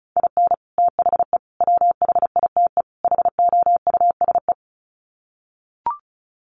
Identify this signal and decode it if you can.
Morse code, heard as a keyed tone, about 34 wpm, 700 Hz. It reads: IN THE WHITE HOUSE
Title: Sets of 4 Words 35wpm